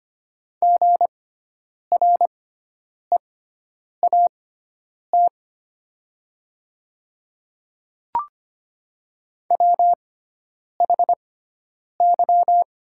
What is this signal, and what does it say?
Morse code, heard as a keyed tone, about 25 wpm, spaced as for 9 wpm, 700 Hz. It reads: GREAT WHY
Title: 6x Character Spacing - Top 200 Words - 25wpm